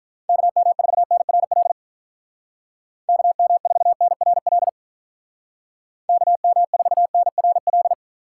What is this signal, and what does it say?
Morse code, heard as a keyed tone, about 43 wpm, 700 Hz. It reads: KM4NRL KM4NRL KM4NRL
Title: Speed Racing - USA 2x3 Call Signs 35wpm